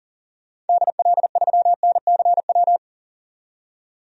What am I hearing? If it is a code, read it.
Morse code, heard as a keyed tone, about 40 wpm, 700 Hz. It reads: DL3NCW
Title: DXCC-100 Call Signs - 6 Characters - 40wpm